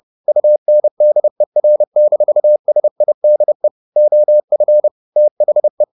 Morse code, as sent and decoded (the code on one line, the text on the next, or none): ..- -. -.. .-. -....- ... .. -.. / --- ..-. / - .... .
UNDER-SIDE OF THE